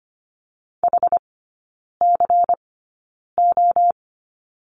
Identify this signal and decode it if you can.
Morse code, heard as a keyed tone, about 25 wpm, spaced as for 9 wpm, 700 Hz. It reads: HCO
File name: course / 25:10wpm / 093 Lesson 16e - Letter c - Three Characters 25wpm